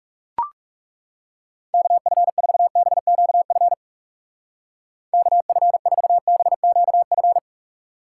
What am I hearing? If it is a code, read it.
Morse code, heard as a keyed tone, about 36 wpm, 700 Hz. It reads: KF4BQF KF4BQF
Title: Speed Racing - USA 2x3 Call Signs 30wpm